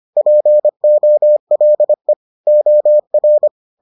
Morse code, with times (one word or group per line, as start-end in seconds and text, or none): 0.2-2.1 POLE
2.5-3.5 OR